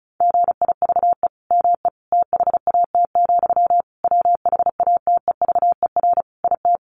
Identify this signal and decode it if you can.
Morse code, heard as a keyed tone, about 34 wpm, 700 Hz. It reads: GIVE ME THAT, WHATEVER IT
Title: Sentences from Top 1100 Words 35wpm